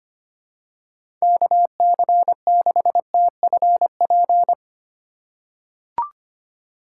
Morse code, heard as a keyed tone, about 25 wpm, 700 Hz. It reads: KC6TFP
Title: USA Call Signs - 3x Character Spacing - 25wpm